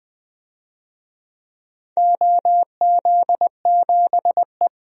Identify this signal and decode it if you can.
Morse code, heard as a keyed tone, about 20 wpm, 700 Hz. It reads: OZ7E
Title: DXCC-100 Call Signs - 4 Characters - Repeated 3x - 20wpm